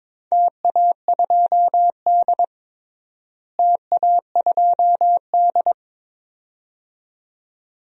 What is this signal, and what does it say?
Morse code, heard as a keyed tone, about 21 wpm, 700 Hz. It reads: TA2D TA2D
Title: DXCC-100 Call Signs - 4 Characters - Repeated 2x - 22wpm